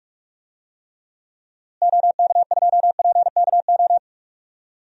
Morse code, heard as a keyed tone, about 46 wpm, 700 Hz. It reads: OK2PKO
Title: DXCC-100 Call Signs - 6 Characters - Repeated 2x - 45wpm